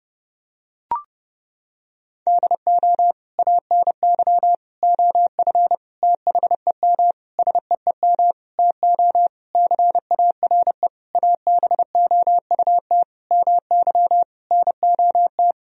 Morse code, heard as a keyed tone, about 32 wpm, 700 Hz. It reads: DO ANY OF THEM SEEM TO CARE ABOUT MY NOT
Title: Sentences from Top 1000 Words - Review Part 3 - 30wpm